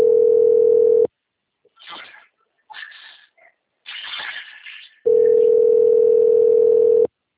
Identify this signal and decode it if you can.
Ringback tone (North America), 440 + 480 Hz.